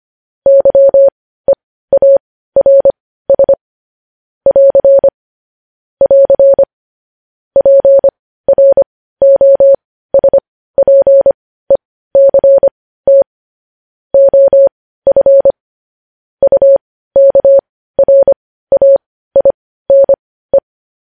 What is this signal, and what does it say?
Morse code, heard as a keyed tone, about 25 wpm, spaced as for 15 wpm, 550 Hz. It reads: YEARS <AR> <AR> PROSPECT OF UKRAINE
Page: news headlines in morse code at 15 wpm